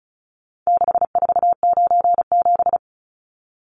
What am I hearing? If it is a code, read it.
Morse code, heard as a keyed tone, about 34 wpm, 700 Hz. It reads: 6497
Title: Numbers - 4-Digit 35wpm